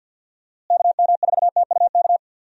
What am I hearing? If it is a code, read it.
Morse code, heard as a keyed tone, about 52 wpm, 700 Hz. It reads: KM4TUK
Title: USA Call Signs - 6 Characters - Repeated 3x - 50wpm